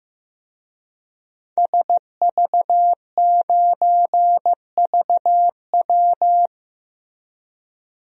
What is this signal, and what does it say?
Morse code, heard as a keyed tone, about 15 wpm, 700 Hz. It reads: SV9VW
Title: DXCC-100 Call Signs - 5 Characters - Repeated 3x - 15wpm